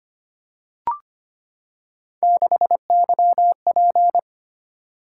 Morse code, heard as a keyed tone, about 25 wpm, 700 Hz. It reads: 6YP
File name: Three Letter-Number Combinations - 25wpm